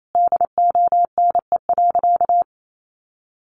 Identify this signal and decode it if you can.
Morse code, heard as a keyed tone, about 28 wpm, 700 Hz. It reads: DONE.
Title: Sentences from Top 500 Words 28wpm